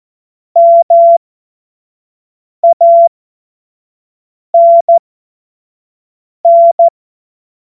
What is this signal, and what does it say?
Morse code, heard as a keyed tone, about 14 wpm, spaced as for 5 wpm, 680 Hz. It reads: MANN
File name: Morsekurs / Witze / Witz 04. Januar 2022, 25 F70